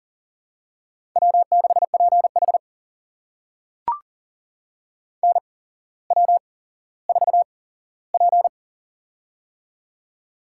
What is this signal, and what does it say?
Morse code, heard as a keyed tone, about 40 wpm, spaced as for 17 wpm, 690 Hz. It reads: W6PH NW4P
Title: USA Call Signs - 4 Characters - 8x Character Spacing - 40wpm